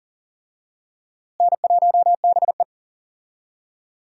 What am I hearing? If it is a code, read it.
Morse code, heard as a keyed tone, about 40 wpm, 700 Hz. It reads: N1BE
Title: USA Call Signs - 4 Characters - Repeated 3x - 40wpm